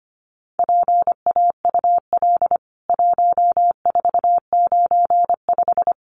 Morse code, heard as a keyed tone, about 25 wpm, 700 Hz. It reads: PAUL 1495
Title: CWT Exchange 25wpm